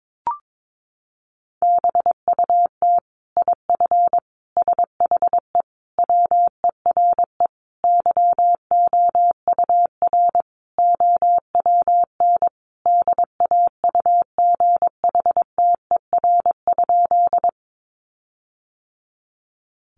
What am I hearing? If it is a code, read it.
Morse code, heard as a keyed tone, about 22 wpm, 700 Hz. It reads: BUT IF SHE WERE YOUR OWN DAUGHTER?
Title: Sentences from Top 1500 Words 22wpm